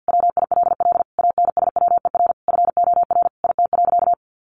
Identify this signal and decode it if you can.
Morse code, heard as a keyed tone, about 50 wpm, 700 Hz. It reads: WILL ANSWER FOR IT.
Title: Sentences from Top 300 Words 50wpm